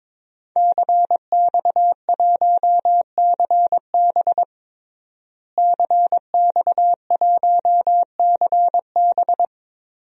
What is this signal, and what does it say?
Morse code, heard as a keyed tone, about 21 wpm, 700 Hz. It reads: CX1CB CX1CB
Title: DXCC-100 Call Signs - 5 Characters - Repeated 3x - 22wpm